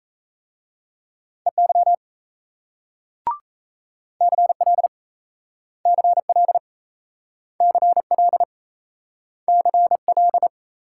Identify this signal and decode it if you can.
Morse code, heard as a keyed tone, about 38 wpm, 700 Hz. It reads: EY CL CL CL CL